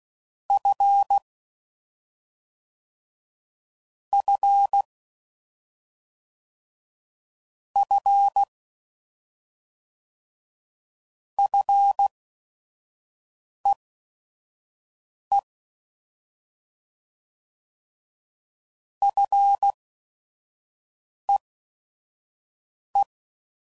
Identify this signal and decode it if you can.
Morse code, heard as a keyed tone, about 16 wpm, spaced as for 5 wpm, 800 Hz. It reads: F F F FEE FEE